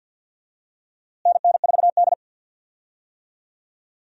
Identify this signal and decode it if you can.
Morse code, heard as a keyed tone, about 52 wpm, 700 Hz. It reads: NN4D
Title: USA Call Signs - 4 Characters - Repeated 3x - 50wpm